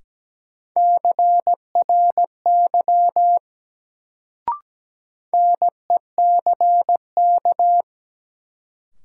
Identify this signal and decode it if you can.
Morse code, heard as a keyed tone, about 17 wpm, 700 Hz. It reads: CRY NECK
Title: Sets of 5 Words Encore - New Word Drill - 17wpm